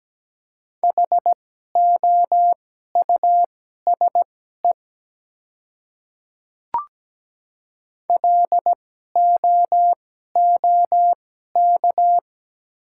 Morse code, heard as a keyed tone, about 17 wpm, spaced as for 12 wpm, 700 Hz. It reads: HOUSE LOOK
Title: 2x Character Spacing - Top 200 Words - 17wpm